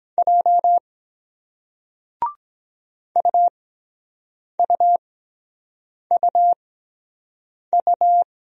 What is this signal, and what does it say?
Morse code, heard as a keyed tone, about 23 wpm, spaced as for 7 wpm, 700 Hz. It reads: J UUUU